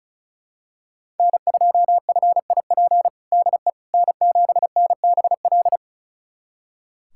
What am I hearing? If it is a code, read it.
Morse code, heard as a keyed tone, about 34 wpm, 690 Hz. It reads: N2FIP DE N7NBL
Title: USA Callsign DE Callsign - 5 Characters - 35wpm